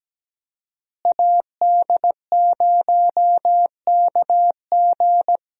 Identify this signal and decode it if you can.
Morse code, heard as a keyed tone, about 17 wpm, 700 Hz. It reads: AD0KG
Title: USA Call Signs - 5 Characters - Repeated 3x - 17wpm